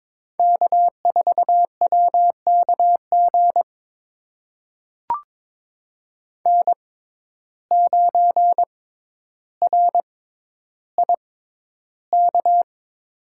Morse code, heard as a keyed tone, about 21 wpm, spaced as for 11 wpm, 700 Hz. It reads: K4WKG N9RIK
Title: USA Call Signs - 5 Characters - 6x Character Spacing - 22wpm